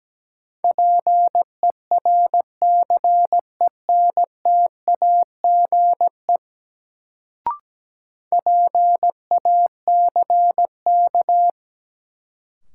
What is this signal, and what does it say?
Morse code, heard as a keyed tone, about 17 wpm, 700 Hz. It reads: PERCENTAGE PACK
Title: Sentences from Top 1300 Words - New Word Drill - 17wpm